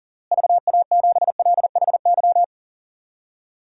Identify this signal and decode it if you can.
Morse code, heard as a keyed tone, about 40 wpm, 700 Hz. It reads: VA7LHY
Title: DXCC-100 Call Signs - 6 Characters - Repeated 3x - 40wpm